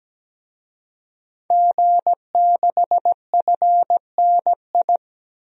Morse code, heard as a keyed tone, about 17 wpm, 700 Hz. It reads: G6FNI
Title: DXCC-100 Call Signs - 5 Characters - Repeated 3x - 17wpm